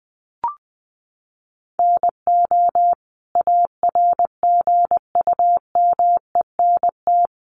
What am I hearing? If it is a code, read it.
Morse code, heard as a keyed tone, about 20 wpm, 700 Hz. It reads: NO ARGUMENT
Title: Sentences from Top 1400 Words 20wpm